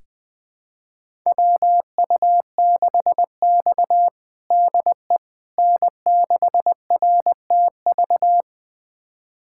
Morse code, heard as a keyed tone, about 20 wpm, 700 Hz. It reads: WU6X DE N6RTV
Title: USA Callsign DE Callsign 20wpm